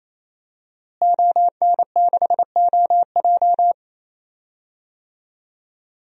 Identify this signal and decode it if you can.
Morse code, heard as a keyed tone, about 28 wpm, 700 Hz. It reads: ON6OJ
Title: DXCC-100 Call Signs - 5 Characters - Repeated 3x - 28wpm